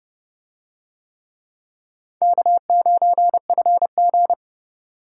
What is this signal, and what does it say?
Morse code, heard as a keyed tone, about 26 wpm, 700 Hz. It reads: K9FG